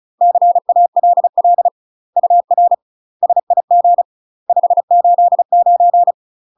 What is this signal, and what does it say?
Morse code, heard as a keyed tone, about 36 wpm, 700 Hz. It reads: CALL UR SIG 589
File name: UK MASTER QSOs TRAINING / 35WPM / UK MASTER QSOs 0017